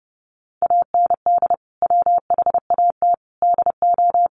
Morse code, heard as a keyed tone, about 30 wpm, 700 Hz. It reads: AND WHAT DO
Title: Sentences from Top 200 Words 30wpm